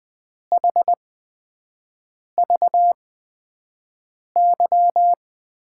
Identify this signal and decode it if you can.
Morse code, heard as a keyed tone, about 20 wpm, spaced as for 5 wpm, 700 Hz. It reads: HVY
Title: USA Call Signs - 5 Characters - 8x Character Spacing - 20wpm